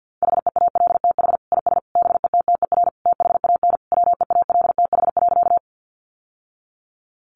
Morse code, heard as a keyed tone, about 48 wpm, 700 Hz. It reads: HEALTH IS BETTER THAN WEALTH.
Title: Sentences from Top 1200 Words 50wpm